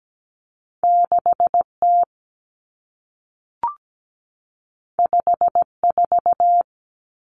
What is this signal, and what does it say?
Morse code, heard as a keyed tone, about 17 wpm, 700 Hz. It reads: 6T 54
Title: Common Cut Numbers - 2-Digit 17wpm